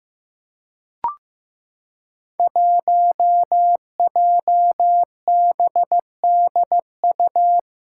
Morse code, heard as a keyed tone, about 15 wpm, 700 Hz. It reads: 1JBDU
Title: Five Challenging Letter-Number Combinations 15wpm